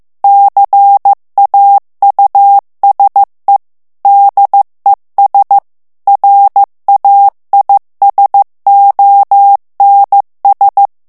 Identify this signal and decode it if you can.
Morse code, heard as a keyed tone, about 15 wpm, 800 Hz. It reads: CAUSE DES RAISONS
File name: morse mp3 / FRANCAIS TEXTE CLAIR 1 VITESSE 14 MOTS MINUTES